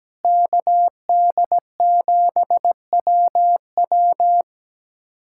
Morse code, heard as a keyed tone, about 17 wpm, 700 Hz. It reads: KD7WW
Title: USA Callsign DE Callsign 17wpm